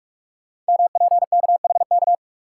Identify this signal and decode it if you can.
Morse code, heard as a keyed tone, about 44 wpm, 700 Hz. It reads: MPKHK
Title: Code Groups - 5 Letter:Numbers - 45wpm